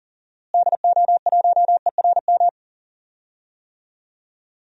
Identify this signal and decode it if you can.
Morse code, heard as a keyed tone, about 40 wpm, 700 Hz. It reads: DO1ERM
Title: DXCC-100 Call Signs - 6 Characters - Repeated 3x - 40wpm